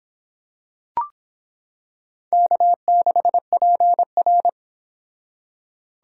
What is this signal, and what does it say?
Morse code, heard as a keyed tone, about 26 wpm, 700 Hz. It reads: K6PR